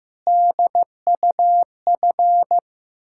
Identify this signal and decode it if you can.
Morse code, heard as a keyed tone, about 15 wpm, 700 Hz. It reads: DUF
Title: Ontario QSO Party Exchange 15wpm